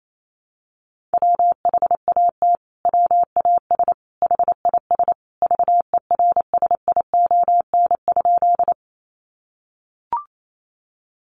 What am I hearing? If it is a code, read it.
Morse code, heard as a keyed tone, about 28 wpm, 700 Hz. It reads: WHAT WAS HIS VERSION?
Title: Sentences from Top 1400 Words 28wpm